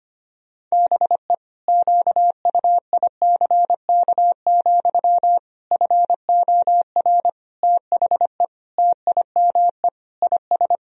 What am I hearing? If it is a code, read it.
Morse code, heard as a keyed tone, about 25 wpm, 700 Hz. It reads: BE QUICK, FOR THE TIME IS
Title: Sentences from Top 1000 Words - Review Part 2 - 25wpm